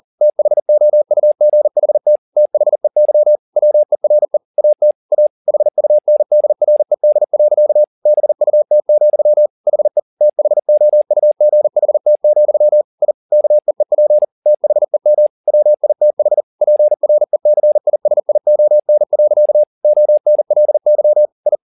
Morse code, heard as a keyed tone, about 40 wpm, 600 Hz. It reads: THOUGHT THEY WERE AT A HUNDRED. BUT, HE THOUGHT, I KEEP THEM WITH PRECISION. ONLY I